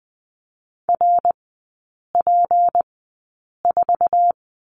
Morse code, heard as a keyed tone, about 20 wpm, spaced as for 8 wpm, 700 Hz. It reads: RP4
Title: 152 Lesson 25e - Letter p - Three Characters 20wpm